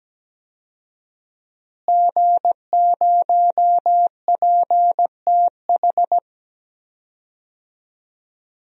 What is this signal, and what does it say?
Morse code, heard as a keyed tone, about 17 wpm, 700 Hz. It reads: G0PTH
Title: DXCC-100 Call Signs - 5 Characters - Repeated 2x - 17wpm